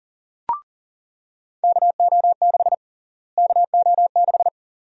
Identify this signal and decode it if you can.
Morse code, heard as a keyed tone, about 40 wpm, 700 Hz. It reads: KO6 KO6
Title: USA Call Sign Prefixes - Repeated 2x - 40wpm